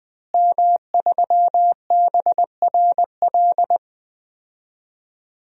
Morse code, heard as a keyed tone, about 21 wpm, 700 Hz. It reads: M3BRL
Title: DXCC-100 Call Signs - 5 Characters - Repeated 3x - 20wpm